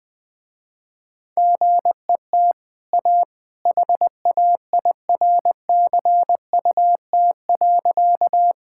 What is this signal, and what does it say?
Morse code, heard as a keyed tone, about 20 wpm, 700 Hz. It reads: GET A HAIRCUT.
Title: Easy Sentences with Hints - 20wpm